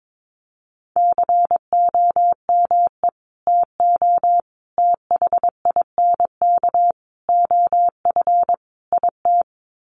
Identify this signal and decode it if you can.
Morse code, heard as a keyed tone, about 22 wpm, 700 Hz. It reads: COME TO THINK OF IT
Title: Sets of 5 Words Encore 22wpm